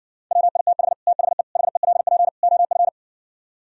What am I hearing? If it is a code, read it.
Morse code, heard as a keyed tone, about 59 wpm, 700 Hz. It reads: WITH THE HELP OF